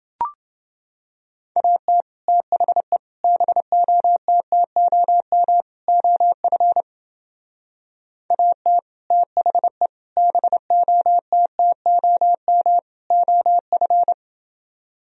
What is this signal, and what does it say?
Morse code, heard as a keyed tone, about 28 wpm, 700 Hz. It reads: AT THE BOTTOM OF AT THE BOTTOM OF